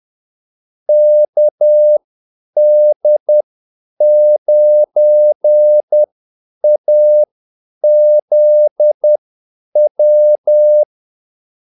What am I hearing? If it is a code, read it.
Morse code, heard as a keyed tone, about 10 wpm, spaced as for 8 wpm, 600 Hz. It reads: KD9AZW
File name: Callsigns-10-8wpm